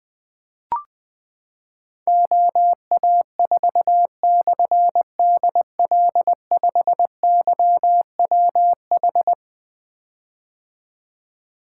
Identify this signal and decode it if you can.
Morse code, heard as a keyed tone, about 20 wpm, 700 Hz. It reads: OA4/DL5YWH
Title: Difficult Call Signs 20wpm